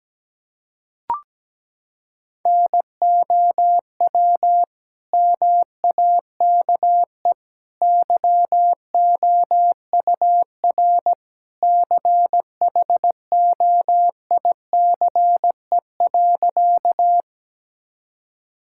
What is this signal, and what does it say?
Morse code, heard as a keyed tone, about 17 wpm, 700 Hz. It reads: NOW MAKE YOUR CHOICE.
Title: Sentences from Top 1000 Words - Review Part 6 - 17wpm